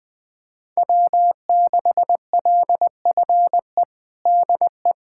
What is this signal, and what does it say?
Morse code, heard as a keyed tone, about 20 wpm, 700 Hz. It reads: W6LFE DE
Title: USA Callsign DE Callsign - 5 Characters - 20wpm